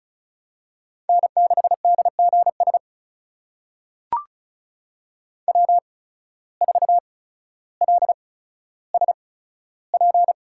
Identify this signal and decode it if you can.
Morse code, heard as a keyed tone, about 36 wpm, spaced as for 15 wpm, 700 Hz. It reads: N6DGS W4LSP
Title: USA Call Signs - 5 Characters - 8x Character Spacing - 35wpm